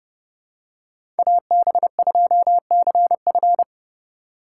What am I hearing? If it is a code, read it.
Morse code, heard as a keyed tone, about 31 wpm, 700 Hz. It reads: AB2CF
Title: USA Call Signs - 5 Characters - 3x Character Spacing - 30wpm